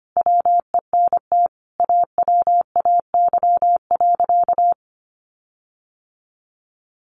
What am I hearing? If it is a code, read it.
Morse code, heard as a keyed tone, about 25 wpm, 700 Hz. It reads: WENT AWAY.
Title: Sentences from Top 1300 Words 25wpm